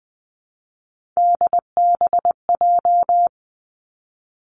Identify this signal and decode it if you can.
Morse code, heard as a keyed tone, about 20 wpm, 700 Hz. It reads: DBJ
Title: Three Challenging Letter-Number Combinations 20wpm